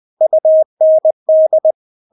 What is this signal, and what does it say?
Morse code, heard as a keyed tone, about 20 wpm, 630 Hz. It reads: UND
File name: Witz1 20wpm